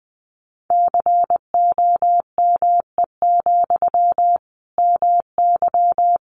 Morse code, heard as a keyed tone, about 20 wpm, 700 Hz. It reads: COME, MY
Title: Sentences from Top 800 Words 20wpm